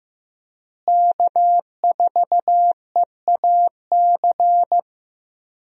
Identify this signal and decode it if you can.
Morse code, heard as a keyed tone, about 15 wpm, 700 Hz. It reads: K4EAC
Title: USA Call Signs - 5 Characters - Repeated 3x - 15wpm